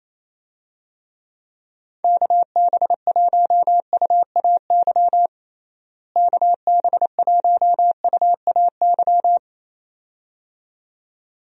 Morse code, heard as a keyed tone, about 27 wpm, 700 Hz. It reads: KB1UAY KB1UAY
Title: USA Call Signs - 6 Characters - Repeated 2x - 28wpm